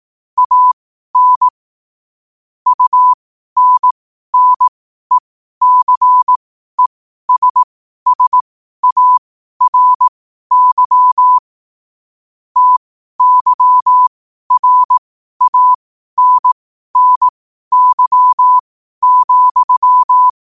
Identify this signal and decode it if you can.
Morse code, heard as a keyed tone, about 18 wpm, spaced as for 12 wpm, 1000 Hz. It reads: AN UNNECESSARY TYRANNY,
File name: qotdfiles / 12wpm